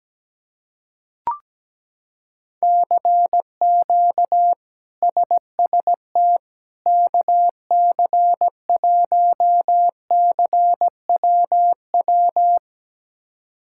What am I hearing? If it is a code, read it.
Morse code, heard as a keyed tone, about 17 wpm, 700 Hz. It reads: CQ SST KC1CWW
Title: K1USN SST QSO Practice - 17wpm